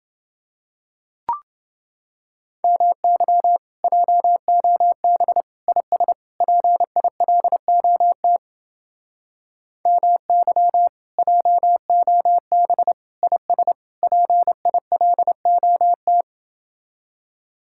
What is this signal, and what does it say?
Morse code, heard as a keyed tone, about 30 wpm, 700 Hz. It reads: MY JOB IS PILOT MY JOB IS PILOT